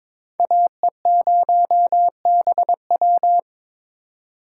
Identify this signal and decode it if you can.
Morse code, heard as a keyed tone, about 22 wpm, 700 Hz. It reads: AE0BW
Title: USA Call Signs - 3x Character Spacing - 22wpm